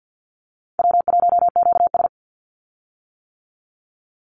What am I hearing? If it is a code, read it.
Morse code, heard as a keyed tone, about 50 wpm, 700 Hz. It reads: W1QS
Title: Easy Call Signs - 50wpm